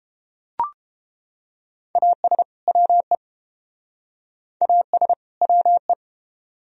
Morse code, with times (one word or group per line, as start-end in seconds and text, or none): 2.0-2.4 AS
2.7-3.1 WE
4.6-5.1 AS
5.4-5.9 WE